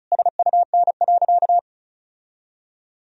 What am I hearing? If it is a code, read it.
Morse code, heard as a keyed tone, about 36 wpm, 700 Hz. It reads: SUN.
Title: Sentences from Top 1000 Words - Review Part 1 - 35wpm